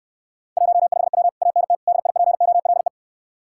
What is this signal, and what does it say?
Morse code, heard as a keyed tone, about 69 wpm, 700 Hz. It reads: 15W ANT DIPOLE